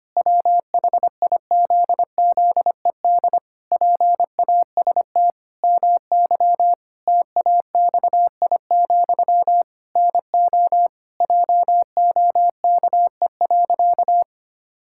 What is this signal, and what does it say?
Morse code, heard as a keyed tone, about 25 wpm, 700 Hz. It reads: WHIZZED PAST MY TAXI, NO JOKE.
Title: Pangram Challenge v2 - 25wpm